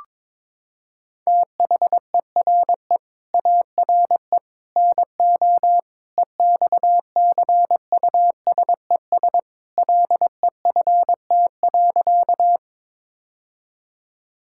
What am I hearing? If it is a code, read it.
Morse code, heard as a keyed tone, about 22 wpm, 700 Hz. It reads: THERE ARE NO EXCUSES LEFT.
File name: practice / 22wpm / Sentences with 3-Word Phrases - 22wpm